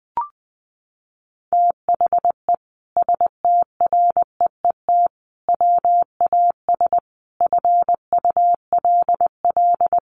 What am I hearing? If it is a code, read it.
Morse code, heard as a keyed tone, about 20 wpm, 700 Hz. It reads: THE STREET WAS FULL
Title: Sentences from Top 1300 Words 20wpm